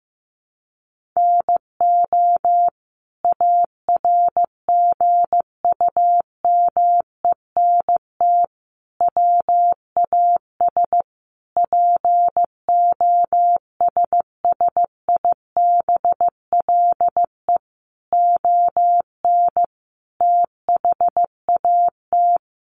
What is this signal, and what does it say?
Morse code, heard as a keyed tone, about 15 wpm, 700 Hz. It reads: NO ARGUMENT WAS POSSIBLE ON THAT
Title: Sentences from Top 1400 Words 15wpm